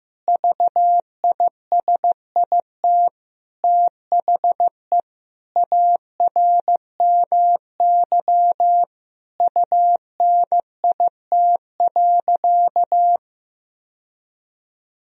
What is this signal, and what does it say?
Morse code, heard as a keyed tone, about 15 wpm, 700 Hz. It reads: VISIT THE ARMY UNIT.
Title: Sentences from Top 1000 Words - Review Part 2 - 15wpm